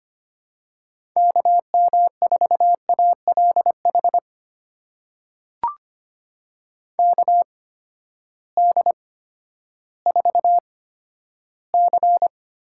Morse code, heard as a keyed tone, about 25 wpm, spaced as for 12 wpm, 700 Hz. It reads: KM4ALH KD4C
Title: USA Call Signs - 6 Characters - 8x Character Spacing - 25wpm